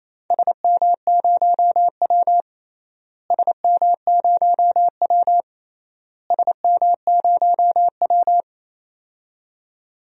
Morse code, heard as a keyed tone, about 28 wpm, 700 Hz. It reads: SM0W SM0W SM0W
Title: DXCC-100 Call Signs - 4 Characters - Repeated 3x - 28wpm